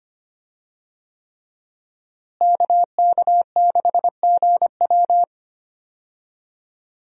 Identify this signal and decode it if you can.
Morse code, heard as a keyed tone, about 25 wpm, 700 Hz. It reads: KK6GW
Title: USA Call Signs - 5 Characters - Repeated 3x - 25wpm